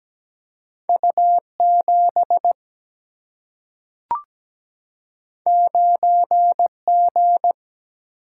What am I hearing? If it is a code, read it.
Morse code, heard as a keyed tone, about 17 wpm, 700 Hz. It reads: U7 9G
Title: Single-Letter Single-Number 17wpm